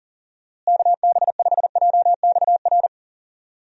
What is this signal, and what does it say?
Morse code, heard as a keyed tone, about 37 wpm, 700 Hz. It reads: KB5JXR
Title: USA Call Signs - 6 Characters - Repeated 3x - 40wpm